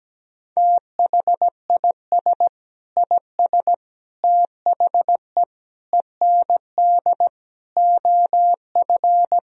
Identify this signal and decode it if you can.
Morse code, heard as a keyed tone, about 17 wpm, 700 Hz. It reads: THIS IS THE END OF
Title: Sentences from Top 1000 Words - Review Part 3 - 17wpm